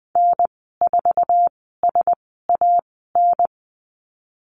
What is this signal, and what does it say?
Morse code, heard as a keyed tone, about 20 wpm, spaced as for 14 wpm, 700 Hz. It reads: N4SAN
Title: 173 Lesson 28g - Number 9 - Call Signs 20wpm